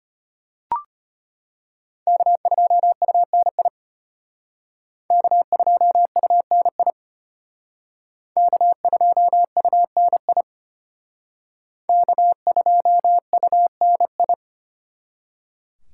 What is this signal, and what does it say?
Morse code, heard as a keyed tone, about 32 wpm, 700 Hz. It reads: K2UNI K2UNI K2UNI K2UNI